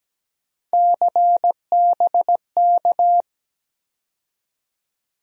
Morse code, heard as a keyed tone, about 17 wpm, 700 Hz. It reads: CBK
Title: USA Call Sign Postfixes - 3 Letters 17wpm